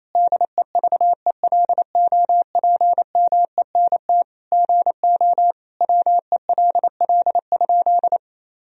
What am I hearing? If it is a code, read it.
Morse code, heard as a keyed tone, about 28 wpm, 700 Hz. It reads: DEVELOPMENT GO WELL?
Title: Sentences from Top 1000 Words - Review Part 4 - 28wpm